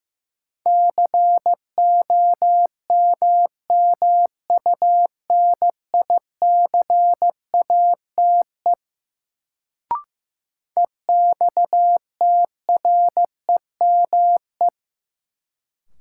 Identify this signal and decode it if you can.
Morse code, heard as a keyed tone, about 15 wpm, 700 Hz. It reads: COMMUNICATE EXTREME
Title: Sentences from Top 1300 Words - New Word Drill - 15wpm